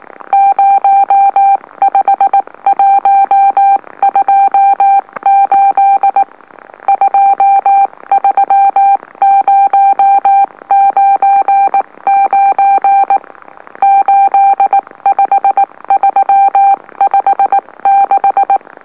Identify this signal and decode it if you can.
Morse code, heard as a keyed tone, about 19 wpm, spaced as for 16 wpm, 790 Hz. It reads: 05128 23099 85356
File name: M04- -2345utc-LOLO